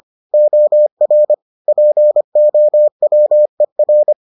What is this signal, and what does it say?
Morse code, heard as a keyed tone, about 25 wpm, 600 Hz. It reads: OR POWER